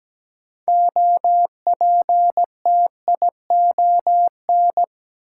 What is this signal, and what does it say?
Morse code, heard as a keyed tone, about 17 wpm, 700 Hz. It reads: OPTION
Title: Sentences from Top 1300 Words - New Word Drill Alt - 17wpm